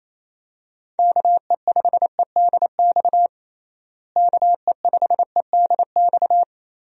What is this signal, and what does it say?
Morse code, heard as a keyed tone, about 28 wpm, 700 Hz. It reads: KE5EDX KE5EDX
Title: USA Call Signs - 6 Characters - Repeated 2x - 28wpm